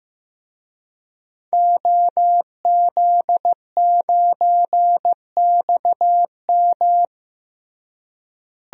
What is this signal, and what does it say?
Morse code, heard as a keyed tone, about 15 wpm, 700 Hz. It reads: OZ9XM
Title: DXCC-100 Call Signs - 5 Characters - Repeated 2x - 15wpm